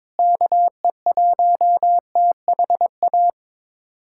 Morse code, heard as a keyed tone, about 22 wpm, 700 Hz. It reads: KE1THA
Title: USA Callsign DE Callsign 22wpm